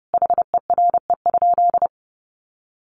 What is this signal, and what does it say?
Morse code, heard as a keyed tone, about 30 wpm, 700 Hz. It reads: HERE?
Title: Sentences from Top 400 Words 30wpm